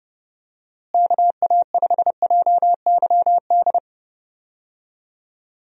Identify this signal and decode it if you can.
Morse code, heard as a keyed tone, about 30 wpm, 700 Hz. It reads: KA5JYD
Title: USA Call Signs - 6 Characters - Repeated 2x - 30wpm